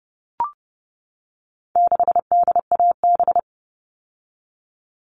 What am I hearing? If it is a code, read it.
Morse code, heard as a keyed tone, about 30 wpm, 700 Hz. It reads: 6DAB
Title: All Cut Numbers - 4-Digit 30wpm